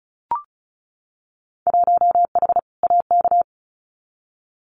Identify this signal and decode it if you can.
Morse code, heard as a keyed tone, about 36 wpm, 700 Hz. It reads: 1H AK
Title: Winter Field Day Exchanges Encore 35wpm